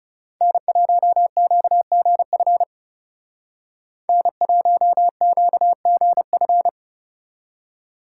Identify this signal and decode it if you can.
Morse code, heard as a keyed tone, about 35 wpm, 700 Hz. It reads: N1QGF N1QGF
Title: Speed Racing - USA 1x3 Call Signs 30wpm